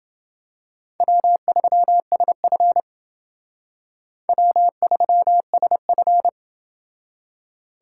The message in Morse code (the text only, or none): W3SF W3SF